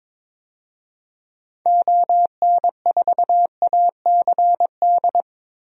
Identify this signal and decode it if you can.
Morse code, heard as a keyed tone, about 22 wpm, 700 Hz. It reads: ON4ACD
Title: DXCC-100 Call Signs - 6 Characters - Repeated 2x - 22wpm